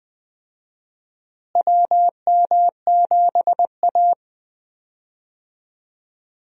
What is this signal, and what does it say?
Morse code, heard as a keyed tone, about 20 wpm, 700 Hz. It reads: WM7A
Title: USA Call Signs - 4 Characters - Repeated 3x - 20wpm